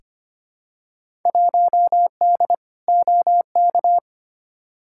Morse code, heard as a keyed tone, about 25 wpm, 700 Hz. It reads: 1D OK
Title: Field Day Exchanges - 25wpm